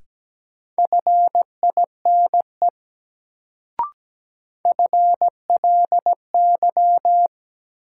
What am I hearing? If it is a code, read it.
Morse code, heard as a keyed tone, about 17 wpm, 700 Hz. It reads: FINE FLY
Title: Sentences from Top 500 Words - New Word Drill - 17wpm